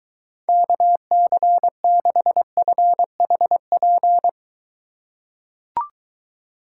Morse code, heard as a keyed tone, about 23 wpm, 700 Hz. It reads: KC6FHP